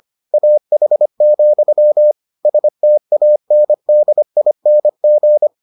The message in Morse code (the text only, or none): AH, STANDING